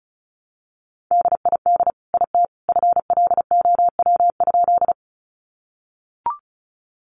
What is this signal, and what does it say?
Morse code, heard as a keyed tone, about 34 wpm, 700 Hz. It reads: DID IT FLOW?
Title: Sentences from Top 1200 Words 35wpm